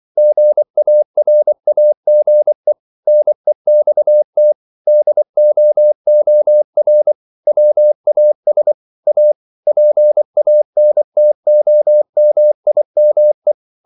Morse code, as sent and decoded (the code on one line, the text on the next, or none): --. .- .-. .- --. / -. -..- - / -.. --- --- .-. / .-- .- ... / .- / .--. .- -. - --- -- .. -- .
GARAGE NEXT DOOR WAS A PANTOMIME